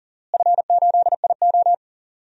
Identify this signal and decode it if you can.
Morse code, heard as a keyed tone, about 40 wpm, 700 Hz. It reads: F8IO
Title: DXCC-100 Call Signs - 4 Characters - Repeated 3x - 40wpm